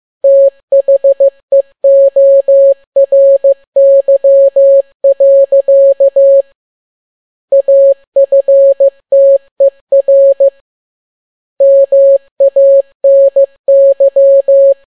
Morse code, heard as a keyed tone, about 15 wpm, 550 Hz. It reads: THEORY. AFTER MANY
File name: n6tt p1 15C13